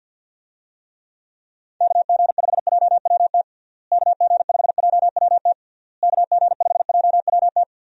Morse code, heard as a keyed tone, about 50 wpm, 700 Hz. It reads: KG5JWT KG5JWT KG5JWT
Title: USA Call Signs - 6 Characters - Repeated 3x - 50wpm